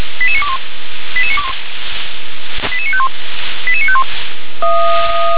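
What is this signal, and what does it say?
Five-tone selective call (ZVEI), heard as quick runs of five tones, tones of about 70 ms. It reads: 88921 88951 88951
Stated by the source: ZVEI five-tone call 88955 received by SDR.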